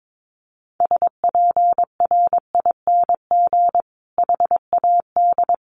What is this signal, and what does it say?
Morse code, heard as a keyed tone, about 22 wpm, 700 Hz. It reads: SPRING HAD
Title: Sentences from Top 1200 Words 22wpm